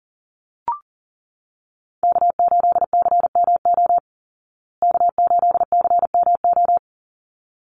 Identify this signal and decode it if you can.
Morse code, heard as a keyed tone, about 40 wpm, 700 Hz. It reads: K8CMO K8CMO
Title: Call Signs Repeated 3x - 40wpm